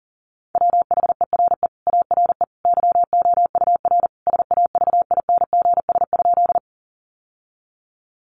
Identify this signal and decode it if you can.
Morse code, heard as a keyed tone, about 40 wpm, 700 Hz. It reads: WHERE ARE YOUR SAVINGS?
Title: Sentences from Top 1200 Words 40wpm